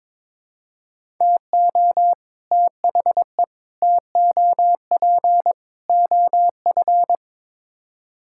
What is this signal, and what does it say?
Morse code, heard as a keyed tone, about 22 wpm, 700 Hz. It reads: TO THE TOP OF
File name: practice / 22wpm / Build A Phrase - 5-Word Phrases - 22wpm